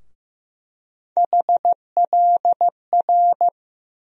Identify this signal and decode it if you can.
Morse code, heard as a keyed tone, about 15 wpm, 700 Hz. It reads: HLR
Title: USA Call Sign Postfixes - 3 Letters 15wpm